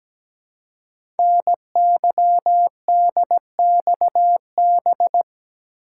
Morse code, heard as a keyed tone, about 17 wpm, 700 Hz. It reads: NYDXB
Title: Code Groups - 5 Letters - 17wpm